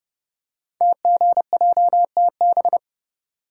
Morse code, heard as a keyed tone, about 30 wpm, 690 Hz. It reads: TGJTB
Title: Code Groups - 5 Letters - 30wpm